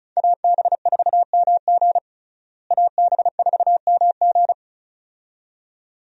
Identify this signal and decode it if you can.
Morse code, heard as a keyed tone, about 36 wpm, 700 Hz. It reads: AB4MG AB4MG
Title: USA Call Signs - 5 Characters - Repeated 2x - 35wpm